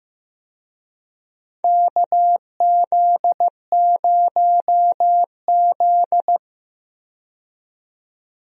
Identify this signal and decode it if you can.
Morse code, heard as a keyed tone, about 15 wpm, 700 Hz. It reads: KZ0Z
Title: USA Call Signs - 4 Characters - 15wpm